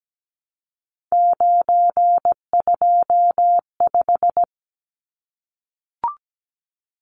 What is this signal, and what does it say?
Morse code, heard as a keyed tone, about 17 wpm, 700 Hz. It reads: 925
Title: Numbers - 3-Digit 17wpm